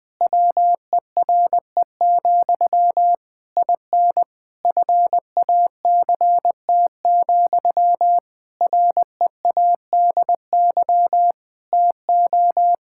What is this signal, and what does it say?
Morse code, heard as a keyed tone, about 20 wpm, 700 Hz. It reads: WERE, IN FACT, READY TO
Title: Sentences from Top 1000 Words - Review Part 5 - 20wpm